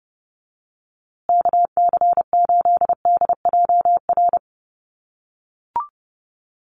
Morse code, heard as a keyed tone, about 30 wpm, 700 Hz. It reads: KC8DJR
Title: USA Call Signs Repeated 2x 30wpm